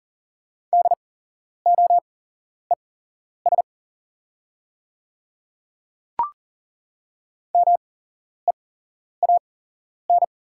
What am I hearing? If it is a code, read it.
Morse code, heard as a keyed tone, about 40 wpm, spaced as for 11 wpm, 700 Hz. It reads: DOES MEAN
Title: 8x Character Spacing - Top 200 Words - 40wpm